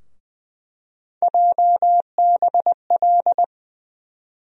Morse code, heard as a keyed tone, about 20 wpm, 700 Hz. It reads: JBL
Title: USA Call Sign Postfixes - 3 Letters 20wpm